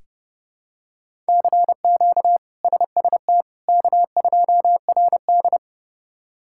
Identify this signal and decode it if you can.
Morse code, heard as a keyed tone, about 32 wpm, 700 Hz. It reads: CQ SST K2RD
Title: K1USN SST QSO Practice - 30wpm